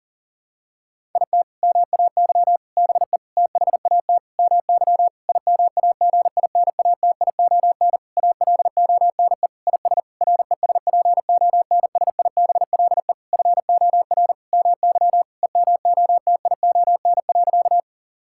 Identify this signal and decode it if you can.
Morse code, heard as a keyed tone, about 40 wpm, 700 Hz. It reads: IT MAY BE THAT MY IMAGINATION ALONE IS RESPONSIBLE FOR MY EMOTION.
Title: Sentences from Top 1600 Words 40wpm